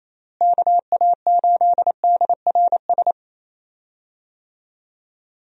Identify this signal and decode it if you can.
Morse code, heard as a keyed tone, about 28 wpm, 700 Hz. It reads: KA8DRS
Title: USA Call Signs - 6 Characters - Repeated 2x - 28wpm